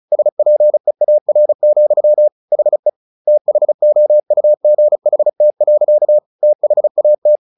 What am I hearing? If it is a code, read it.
Morse code, heard as a keyed tone, about 34 wpm, 600 Hz. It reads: SPEAR, HE THOUGHT. THAT
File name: the old man and the sea 2 35wpm